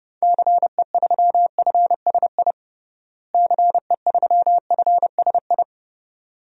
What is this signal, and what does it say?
Morse code, heard as a keyed tone, about 32 wpm, 700 Hz. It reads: CE3FSI CE3FSI
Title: DXCC-100 Call Signs - 6 Characters - Repeated 2x - 30wpm